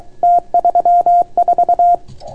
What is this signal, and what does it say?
Morse code, heard as a keyed tone, about 29 wpm, 690 Hz. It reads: T34